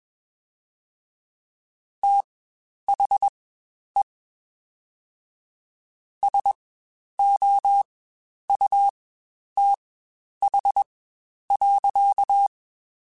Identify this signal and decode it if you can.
Morse code, heard as a keyed tone, about 21 wpm, spaced as for 10 wpm, 800 Hz. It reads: THE SOUTH.
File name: Spaced Characters / Spaced Characters-mp3s / Weather 4 10 sp @20